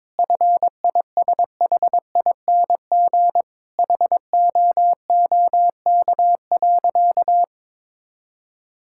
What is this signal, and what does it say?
Morse code, heard as a keyed tone, about 22 wpm, 700 Hz. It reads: FISHING HOOK.
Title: Sentences from Top 1600 Words 22wpm